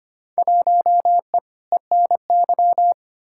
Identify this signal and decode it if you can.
Morse code, heard as a keyed tone, about 25 wpm, 700 Hz. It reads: 1E ENY
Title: Field Day Exchanges - 25wpm